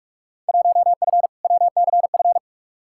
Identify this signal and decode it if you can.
Morse code, heard as a keyed tone, about 45 wpm, 700 Hz. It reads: 1F WCF
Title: Field Day Exchanges - 45wpm